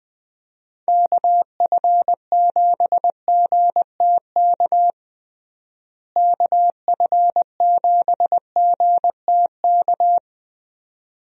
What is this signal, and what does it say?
Morse code, heard as a keyed tone, about 20 wpm, 700 Hz. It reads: KF7GTK KF7GTK
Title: USA Call Signs - 6 Characters - Repeated 2x - 20wpm